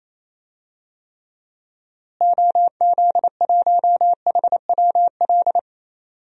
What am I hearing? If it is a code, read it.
Morse code, heard as a keyed tone, about 28 wpm, 700 Hz. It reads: OZ1HWL
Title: DXCC-100 Call Signs - 6 Characters - Repeated 3x - 28wpm